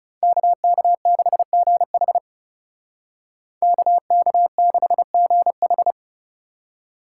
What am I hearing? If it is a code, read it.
Morse code, heard as a keyed tone, about 37 wpm, 700 Hz. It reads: KK6GH KK6GH
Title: Speed Racing - USA 2x2 Call Signs 30wpm